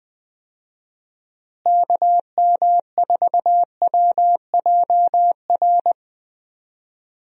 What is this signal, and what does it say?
Morse code, heard as a keyed tone, about 21 wpm, 700 Hz. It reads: KM4WJR
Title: USA Call Signs - 6 Characters - Repeated 2x - 20wpm